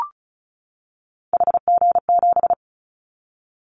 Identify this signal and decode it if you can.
Morse code, heard as a keyed tone, about 36 wpm, 690 Hz. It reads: HG7
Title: Callsign Prefixes for Top 50 DXCC - 35wpm